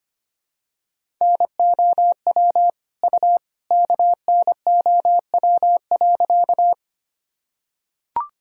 Morse code, heard as a keyed tone, about 25 wpm, 700 Hz. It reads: NOW U KNOW.
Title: Sentences from Top 100 Words and Abbreviations 25wpm